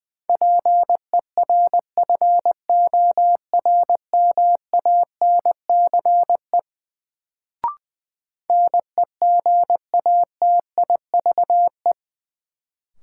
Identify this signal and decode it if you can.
Morse code, heard as a keyed tone, about 20 wpm, 700 Hz. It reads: PERFORMANCE NEGATIVE
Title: Sentences from Top 1000 Words - New Word Drill - 20wpm